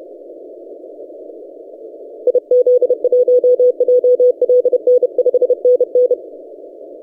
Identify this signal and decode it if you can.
Morse code, heard as a keyed tone, about 30 wpm, 500 Hz. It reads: IZ1JLN 5NN
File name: posts / 4w-n1yc-17cw- -1516z